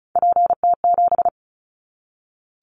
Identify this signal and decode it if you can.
Morse code, heard as a keyed tone, about 36 wpm, 700 Hz. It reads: PT7
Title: Callsign Prefixes for Top 100 DXCC - 35wpm